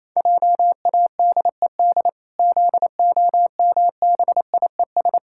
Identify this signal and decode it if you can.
Morse code, heard as a keyed tone, about 28 wpm, 700 Hz. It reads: JADED ZOMBIES
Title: Pangram Challenge v2 - 28wpm